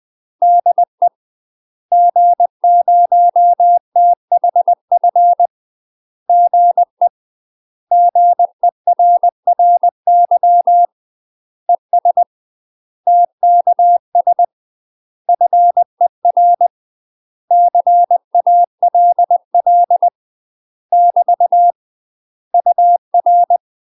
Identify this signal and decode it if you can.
Morse code, heard as a keyed tone, about 20 wpm, 700 Hz. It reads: DE G0THF GE GERRY ES TKS FER CALL <BT> UR
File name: UK MASTER QSOs TRAINING / 20WPM / UK MASTER QSOs 0008